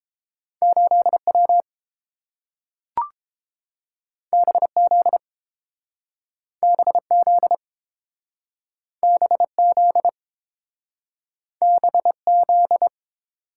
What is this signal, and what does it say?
Morse code, heard as a keyed tone, about 30 wpm, 700 Hz. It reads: 8W BZ BZ BZ BZ